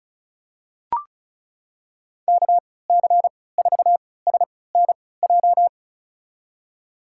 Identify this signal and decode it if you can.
Morse code, heard as a keyed tone, about 34 wpm, spaced as for 20 wpm, 700 Hz. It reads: KC4SNJ
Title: USA Call Signs - 6 Characters - 3x Character Spacing - 35wpm